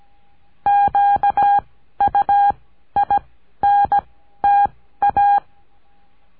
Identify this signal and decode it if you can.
Morse code, heard as a keyed tone, about 17 wpm, spaced as for 12 wpm, 810 Hz. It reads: QUINTA
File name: cursos / CURSO DE TELEGRAFIA / Telegrafia - Trilha 14